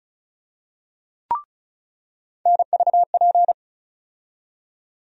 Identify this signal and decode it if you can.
Morse code, heard as a keyed tone, about 34 wpm, 700 Hz. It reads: NVP
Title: USA Call Sign Postfixes - 3 Letters 35wpm